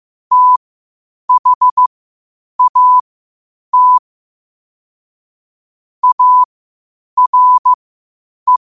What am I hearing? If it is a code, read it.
Morse code, heard as a keyed tone, about 15 wpm, spaced as for 8 wpm, 1000 Hz. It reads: THAT ARE
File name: qotdfiles / 8wpm